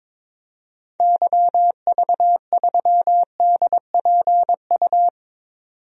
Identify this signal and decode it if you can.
Morse code, heard as a keyed tone, about 22 wpm, 700 Hz. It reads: YV3DPU
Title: DXCC-100 Call Signs - 6 Characters - 22wpm